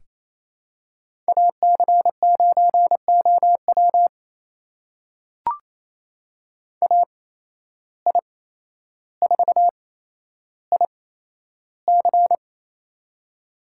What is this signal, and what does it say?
Morse code, heard as a keyed tone, about 28 wpm, spaced as for 12 wpm, 700 Hz. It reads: AC9OW AI4IC
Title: USA Call Signs - 5 Characters - 8x Character Spacing - 28wpm